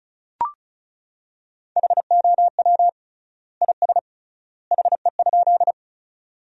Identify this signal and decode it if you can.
Morse code, heard as a keyed tone, about 36 wpm, 700 Hz. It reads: HOW IS HE?
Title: 3x Word-Spacing - Sentences from Top 100 Words - 35wpm